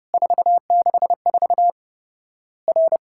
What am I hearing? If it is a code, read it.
Morse code, heard as a keyed tone, about 30 wpm, 700 Hz. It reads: 464 R
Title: Worked All Europe DX Contest - QTC Exchange - 30wpm